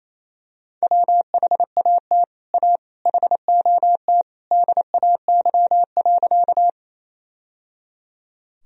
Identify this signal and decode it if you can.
Morse code, heard as a keyed tone, about 28 wpm, 700 Hz. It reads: WHAT A HOT DAY.
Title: Sentences from Top 1000 Words - Review Part 3 - 28wpm